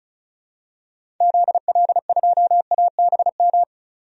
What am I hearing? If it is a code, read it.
Morse code, heard as a keyed tone, about 36 wpm, 700 Hz. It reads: ZL2ABM
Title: DXCC-100 Call Signs - 6 Characters - 35wpm